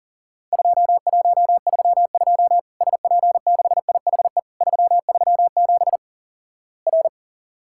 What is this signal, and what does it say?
Morse code, heard as a keyed tone, about 37 wpm, 700 Hz. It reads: 2132 SP6IHE 337 R
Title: Worked All Europe DX Contest - QTC Exchange - 40wpm